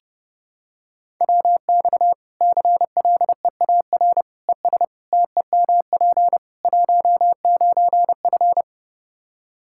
Morse code, heard as a keyed tone, about 30 wpm, 700 Hz. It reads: WX CLEAR ES TEMP 19F
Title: Practice QSO - Exchange 2 - Info - 30wpm